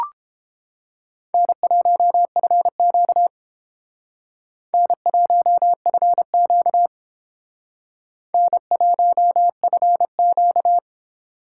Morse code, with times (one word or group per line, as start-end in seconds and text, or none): 1.3-3.3 N1FQ
4.7-6.9 N1FQ
8.3-10.8 N1FQ